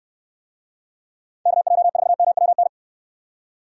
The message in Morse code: KJ4MWN

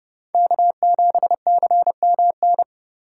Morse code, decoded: K7CMN

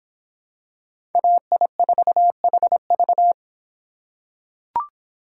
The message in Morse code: AI4HV

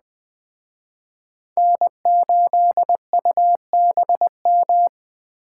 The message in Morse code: N8UBM

